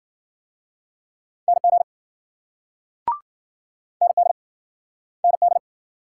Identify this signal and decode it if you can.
Morse code, heard as a keyed tone, about 49 wpm, 700 Hz. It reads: NG ND ND
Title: Speed-Racing Multi-letter Phonemes - 40wpm